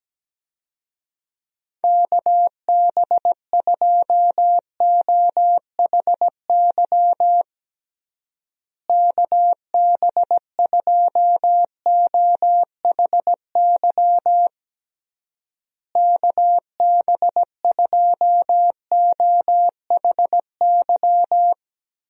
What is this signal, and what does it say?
Morse code, heard as a keyed tone, about 17 wpm, 700 Hz. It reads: KB2OHY KB2OHY KB2OHY
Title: USA Call Signs - 6 Characters - Repeated 3x - 17wpm